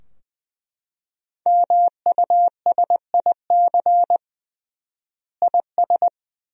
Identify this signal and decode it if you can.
Morse code, heard as a keyed tone, about 20 wpm, 700 Hz. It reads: MUSIC IS
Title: 3x Word-Spacing - Sentences from Top 300 Words - 20wpm